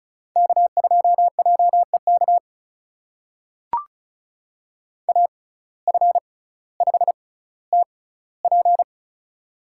Morse code, heard as a keyed tone, about 36 wpm, spaced as for 18 wpm, 700 Hz. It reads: K2JEK AF5TP
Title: USA Call Signs - 5 Characters - 6x Character Spacing - 35wpm